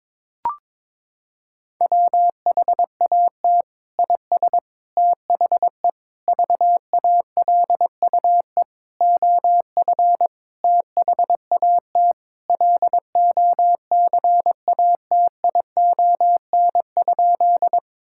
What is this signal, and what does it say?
Morse code, heard as a keyed tone, about 22 wpm, 700 Hz. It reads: WHAT IS THE VALUE OF THAT LOCATION?
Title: Sentences from Top 1000 Words - Review Part 6 - 22wpm